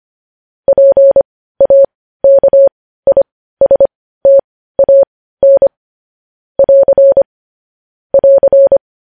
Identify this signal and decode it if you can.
Morse code, heard as a keyed tone, about 25 wpm, spaced as for 15 wpm, 550 Hz. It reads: PAKISTAN <AR> <AR>
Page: News headlines at 15